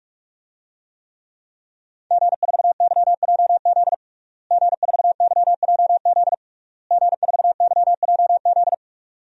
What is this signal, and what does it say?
Morse code, heard as a keyed tone, about 46 wpm, 700 Hz. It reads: G4YJZ G4YJZ G4YJZ
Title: DXCC-100 Call Signs - 5 Characters - Repeated 3x - 45wpm